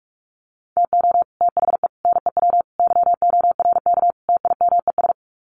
Morse code, heard as a keyed tone, about 44 wpm, 700 Hz. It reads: TO THE NEW YORK TIMES
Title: Sets of 5 Words Encore2 45wpm